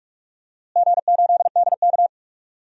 Morse code, heard as a keyed tone, about 46 wpm, 700 Hz. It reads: G8DK
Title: DXCC-100 Call Signs - 4 Characters - Repeated 3x - 45wpm